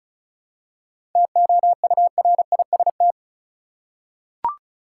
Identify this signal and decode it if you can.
Morse code, heard as a keyed tone, about 36 wpm, 700 Hz. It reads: TOURIST
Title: Seven Letter Words - 35wpm